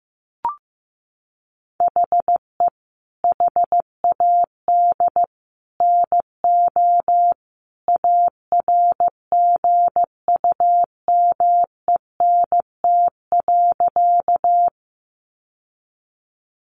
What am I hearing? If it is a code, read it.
Morse code, heard as a keyed tone, about 15 wpm, 700 Hz. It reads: HE HAD NO ARGUMENT.
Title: Sentences from Top 1400 Words 15wpm